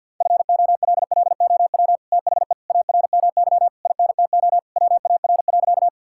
Morse code, heard as a keyed tone, about 49 wpm, 700 Hz. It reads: FOLLOW THE ARMY INTO WAR.
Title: Sentences from Top 1000 Words - Review Part 5 - 50wpm